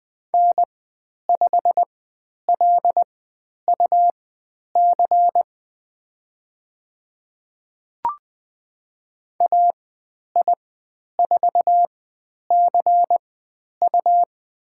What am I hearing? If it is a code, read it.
Morse code, heard as a keyed tone, about 20 wpm, spaced as for 10 wpm, 700 Hz. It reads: N5LUC AI4CU
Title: 102 Lesson 17g - Letter u - Call Signs 20wpm